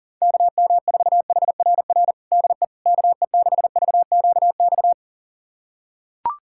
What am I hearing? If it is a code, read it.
Morse code, heard as a keyed tone, about 40 wpm, 700 Hz. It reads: KM4HRR DE KE6VQX
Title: USA Callsign DE Callsign 40wpm